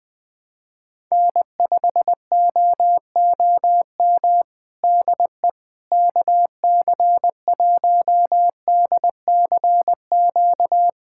Morse code, heard as a keyed tone, about 20 wpm, 700 Hz. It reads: N5OOM DE KC1DCQ
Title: USA Callsign DE Callsign 20wpm